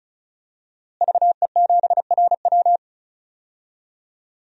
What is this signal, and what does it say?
Morse code, heard as a keyed tone, about 34 wpm, 700 Hz. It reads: VE7RW
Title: DXCC-100 Call Signs - 5 Characters - Repeated 3x - 35wpm